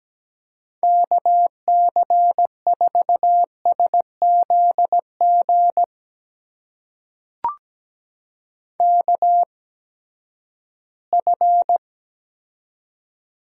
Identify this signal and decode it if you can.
Morse code, heard as a keyed tone, about 17 wpm, spaced as for 12 wpm, 700 Hz. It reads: KC4SZG KF
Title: USA Call Signs - 6 Characters - 8x Character Spacing - 17wpm